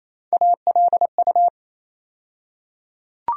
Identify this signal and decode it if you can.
Morse code, heard as a keyed tone, about 28 wpm, 700 Hz. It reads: ALU
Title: USA Call Sign Postfixes - 3 Letters 28wpm